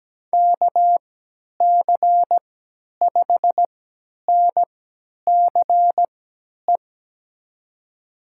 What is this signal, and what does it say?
Morse code, heard as a keyed tone, about 17 wpm, spaced as for 10 wpm, 700 Hz. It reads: KC5NCE
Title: USA Call Signs - 6 Characters - 3x Character Spacing - 17wpm